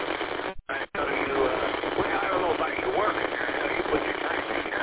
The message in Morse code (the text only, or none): none